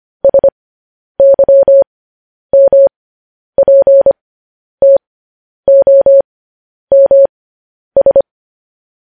Morse code, heard as a keyed tone, about 25 wpm, spaced as for 10 wpm, 550 Hz. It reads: SYMPTOMS
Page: news headlines in morse code at 10 wpm